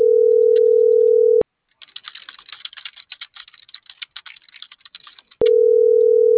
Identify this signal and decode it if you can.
Ringback tone (North America), 440 + 480 Hz.